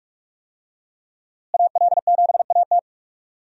Morse code, heard as a keyed tone, about 46 wpm, 700 Hz. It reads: AL7AT
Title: USA Call Signs - 5 Characters - Repeated 3x - 45wpm